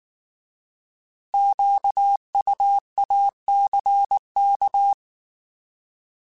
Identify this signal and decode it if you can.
Morse code, heard as a keyed tone, about 19 wpm, 800 Hz. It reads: QUACK
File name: Alphabetical / Alphabetical mp3s / Qs @18